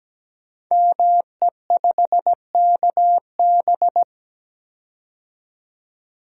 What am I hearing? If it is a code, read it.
Morse code, heard as a keyed tone, about 17 wpm, 700 Hz. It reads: ME5KB
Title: Code Groups - 5 Letter:Numbers - 17wpm